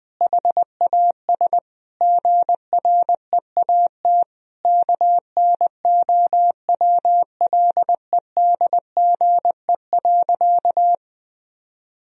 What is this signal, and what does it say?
Morse code, heard as a keyed tone, about 20 wpm, 700 Hz. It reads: HAS GREAT KNOWLEDGE.
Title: Sentences from Top 1000 Words - Review Part 3 - 20wpm